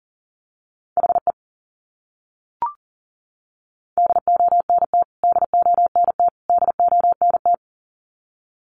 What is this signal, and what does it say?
Morse code, heard as a keyed tone, about 40 wpm, 700 Hz. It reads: HE DONT DONT DONT
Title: Top 500 Words Repeated 40wpm